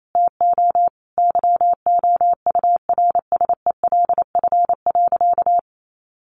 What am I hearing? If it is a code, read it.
Morse code, heard as a keyed tone, about 28 wpm, 700 Hz. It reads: TO YOURSELF.
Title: Sentences from Top 1100 Words 28wpm